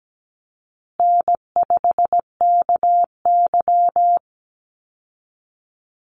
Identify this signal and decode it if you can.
Morse code, heard as a keyed tone, about 17 wpm, 700 Hz. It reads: N5KY
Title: Easy Call Signs Repeated 3x - 17wpm